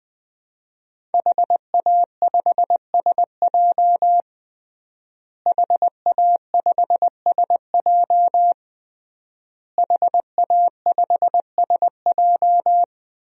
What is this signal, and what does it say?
Morse code, heard as a keyed tone, about 20 wpm, 700 Hz. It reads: HA5SJ HA5SJ HA5SJ
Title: DXCC-100 Call Signs - 5 Characters - Repeated 3x - 20wpm